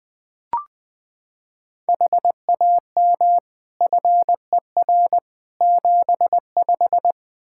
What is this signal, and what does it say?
Morse code, heard as a keyed tone, about 20 wpm, 700 Hz. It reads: HAM FER 75
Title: Practice QSO - Exchange 3 - More Info - 20wpm